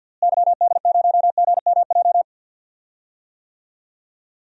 Morse code, heard as a keyed tone, about 46 wpm, 700 Hz. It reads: YD0GMJ
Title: DXCC-100 Call Signs - 6 Characters - Repeated 2x - 50wpm